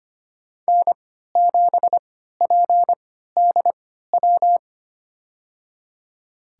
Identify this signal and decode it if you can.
Morse code, heard as a keyed tone, about 25 wpm, spaced as for 14 wpm, 700 Hz. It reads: N7PDW
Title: USA Call Signs - 5 Characters - 3x Character Spacing - 25wpm